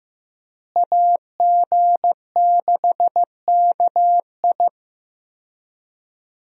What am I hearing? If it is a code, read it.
Morse code, heard as a keyed tone, about 15 wpm, 700 Hz. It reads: AG6KI
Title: USA Call Signs - 5 Characters - Repeated 3x - 15wpm